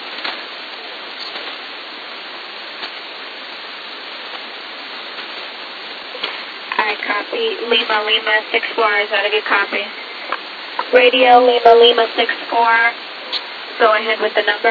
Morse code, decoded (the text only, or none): none